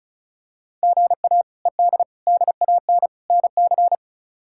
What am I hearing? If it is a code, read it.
Morse code, heard as a keyed tone, about 34 wpm, 700 Hz. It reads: GA ED DAN NC
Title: K1USN SST QSO Practice - 35wpm